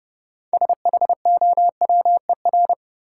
Morse code, heard as a keyed tone, about 30 wpm, 700 Hz. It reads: SHOWER
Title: Sentences from Top 1600 Words - New Word Drill - 30wpm